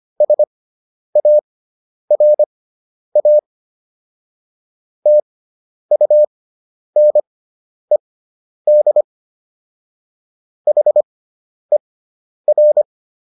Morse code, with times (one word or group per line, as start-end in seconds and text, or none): 0.2-3.4 SARA
5.1-9.0 TUNED
10.7-12.8 HER